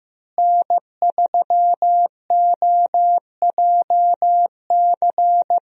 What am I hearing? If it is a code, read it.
Morse code, heard as a keyed tone, about 15 wpm, 700 Hz. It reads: N3OJC
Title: USA Call Signs - 5 Characters - Repeated 2x - 15wpm